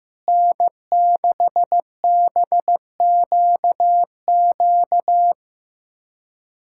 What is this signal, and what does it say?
Morse code, heard as a keyed tone, about 15 wpm, 700 Hz. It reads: N6BQQ
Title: K1USN SST QSO Practice - 15wpm